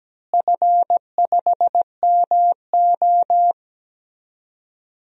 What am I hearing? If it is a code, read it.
Morse code, heard as a keyed tone, about 17 wpm, 700 Hz. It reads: F5MO
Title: DXCC-100 Call Signs - 4 Characters - Repeated 2x - 17wpm